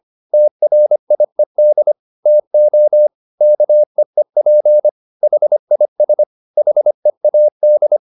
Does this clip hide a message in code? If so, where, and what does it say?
Yes: Morse code from 0.3 to 8.0 s, heard as a keyed tone, about 25 wpm, 600 Hz. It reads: TRIED TO KEEP HIS HEAD